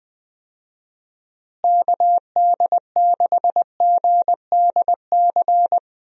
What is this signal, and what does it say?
Morse code, heard as a keyed tone, about 21 wpm, 700 Hz. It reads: KD6GDC
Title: USA Call Signs - 6 Characters - Repeated 2x - 20wpm